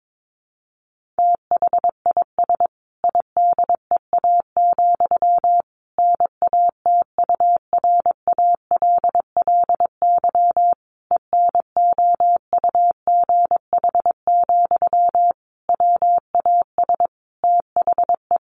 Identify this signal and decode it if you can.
Morse code, heard as a keyed tone, about 22 wpm, 700 Hz. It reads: THIS IDEA, NATURALLY ENOUGH, WAS THE
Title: Sentences from Top 1300 Words 22wpm